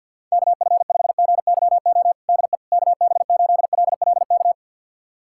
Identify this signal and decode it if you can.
Morse code, heard as a keyed tone, about 50 wpm, 700 Hz. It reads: KF5GYO DE KB8LLK
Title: USA Callsign DE Callsign 50wpm